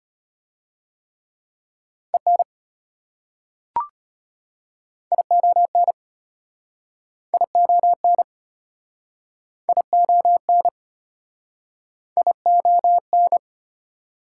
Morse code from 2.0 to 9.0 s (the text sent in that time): EN ION ION